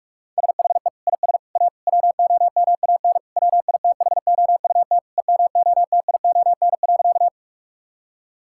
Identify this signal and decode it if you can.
Morse code, heard as a keyed tone, about 41 wpm, 700 Hz. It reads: SHE IS A WOMAN WITHOUT EMOTION.
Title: Sentences from Top 1600 Words 45wpm